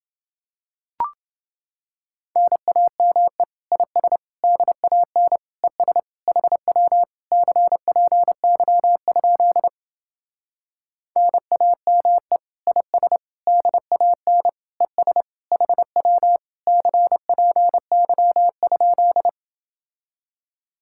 Morse code, heard as a keyed tone, about 30 wpm, 700 Hz. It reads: NAME IS DAN ES HW CPY? NAME IS DAN ES HW CPY?